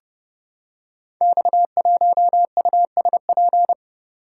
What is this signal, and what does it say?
Morse code, heard as a keyed tone, about 29 wpm, 700 Hz. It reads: X1USP
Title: Code Groups - 5 Letter:Numbers - 30wpm